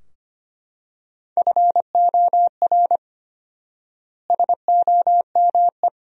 Morse code, heard as a keyed tone, about 25 wpm, 700 Hz. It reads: FOR SOME
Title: 4x Word-Spacing - Sentences from Top 200 Words - 25wpm